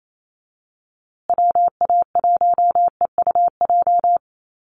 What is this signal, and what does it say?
Morse code, heard as a keyed tone, about 28 wpm, 700 Hz. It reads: WA1EUJ
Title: USA Call Signs Repeated 3x 28wpm